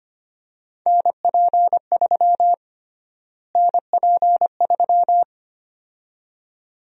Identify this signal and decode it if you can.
Morse code, heard as a keyed tone, about 25 wpm, 700 Hz. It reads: NP3 NP3
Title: USA Call Sign Prefixes - Repeated 2x - 25wpm